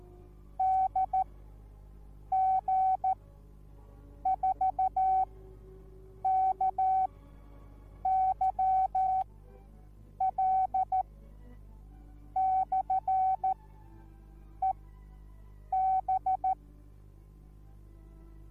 Morse code, heard as a keyed tone, about 14 wpm, spaced as for 6 wpm, 760 Hz. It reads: DG4KYL/EB